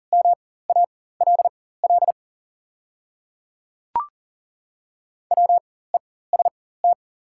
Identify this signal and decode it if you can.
Morse code, heard as a keyed tone, about 40 wpm, spaced as for 19 wpm, 690 Hz. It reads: MALL WEST
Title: 4x Character Spacing - Top 500 Words - 40wpm